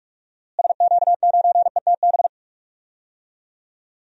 Morse code, heard as a keyed tone, about 45 wpm, 700 Hz. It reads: SQ9ETB
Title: DXCC-100 Call Signs - 6 Characters - Repeated 2x - 45wpm